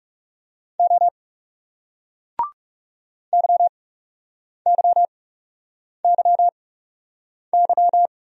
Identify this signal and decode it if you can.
Morse code, heard as a keyed tone, about 42 wpm, spaced as for 8 wpm, 700 Hz. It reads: O YYYY